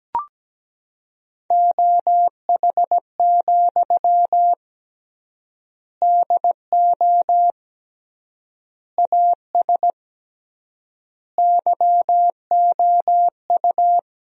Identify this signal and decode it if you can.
Morse code, heard as a keyed tone, about 17 wpm, 700 Hz. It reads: OH, DO AS YOU
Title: 3x Word-Spacing - Sentences from Top 500 Words - 17wpm